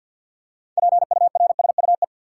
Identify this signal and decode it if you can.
Morse code, heard as a keyed tone, about 51 wpm, 700 Hz. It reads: PURSUE
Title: Six Letter Words - 50wpm